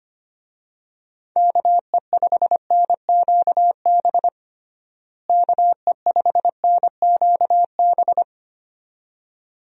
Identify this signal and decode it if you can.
Morse code, heard as a keyed tone, about 25 wpm, 700 Hz. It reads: KE5NQB KE5NQB
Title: USA Call Signs - 6 Characters - Repeated 2x - 25wpm